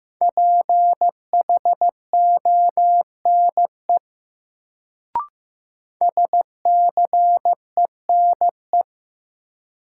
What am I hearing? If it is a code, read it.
Morse code, heard as a keyed tone, about 15 wpm, 700 Hz. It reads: PHONE SCENE
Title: Sentences from Top 1100 Words - New Word Drill - 15wpm